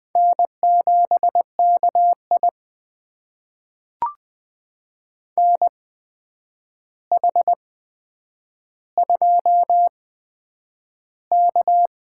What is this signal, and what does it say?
Morse code, heard as a keyed tone, about 21 wpm, spaced as for 9 wpm, 700 Hz. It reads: N7KI NH2K
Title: USA Call Signs - 4 Characters - 8x Character Spacing - 20wpm